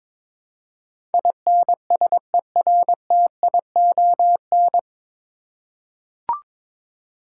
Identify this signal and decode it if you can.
Morse code, heard as a keyed tone, about 23 wpm, 700 Hz. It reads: INSERTION
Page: Morse Code Ninja practice 22wpm